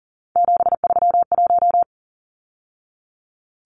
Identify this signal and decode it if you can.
Morse code, heard as a keyed tone, about 40 wpm, 700 Hz. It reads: 731
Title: Numbers - 3-Digit 40wpm